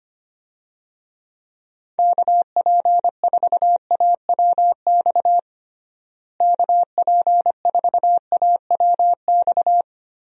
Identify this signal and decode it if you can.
Morse code, heard as a keyed tone, about 25 wpm, 700 Hz. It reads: KP4AWX KP4AWX
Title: DXCC-100 Call Signs - 6 Characters - Repeated 3x - 25wpm